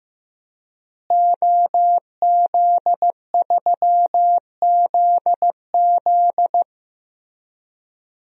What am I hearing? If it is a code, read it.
Morse code, heard as a keyed tone, about 15 wpm, 700 Hz. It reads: OZ3ZZ
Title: DXCC-100 Call Signs - 5 Characters - Repeated 3x - 15wpm